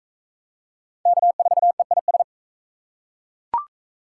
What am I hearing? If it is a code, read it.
Morse code, heard as a keyed tone, about 38 wpm, 700 Hz. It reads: K4EIS